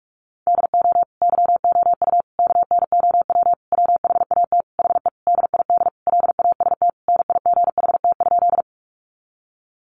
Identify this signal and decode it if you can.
Morse code, heard as a keyed tone, about 44 wpm, 700 Hz. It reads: DO YOU KNOW WHAT HE DID LAST NIGHT?
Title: Sentences from Top 300 Words 45wpm